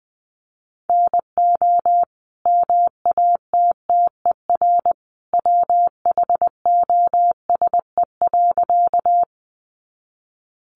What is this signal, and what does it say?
Morse code, heard as a keyed tone, about 20 wpm, 700 Hz. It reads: NO MATTER WHOSE.
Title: Sentences from Top 1200 Words 20wpm